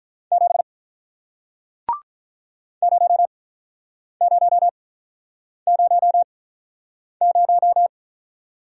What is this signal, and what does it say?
Morse code, heard as a keyed tone, about 48 wpm, spaced as for 9 wpm, 700 Hz. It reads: 7 0000